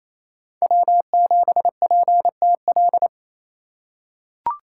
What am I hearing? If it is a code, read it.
Morse code, heard as a keyed tone, about 28 wpm, 700 Hz. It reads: W7PTL
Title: USA Call Signs - 5 Characters - 2x Character Spacing - 28wpm